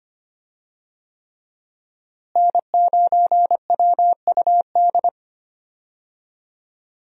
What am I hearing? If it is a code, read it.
Morse code, heard as a keyed tone, about 25 wpm, 700 Hz. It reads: N9WUD
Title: USA Call Signs - 5 Characters - Repeated 3x - 25wpm